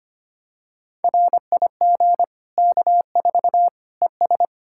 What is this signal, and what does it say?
Morse code, heard as a keyed tone, about 25 wpm, 700 Hz. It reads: RIG K4 ES
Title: Practice QSO - Exchange 2 - Info - 25wpm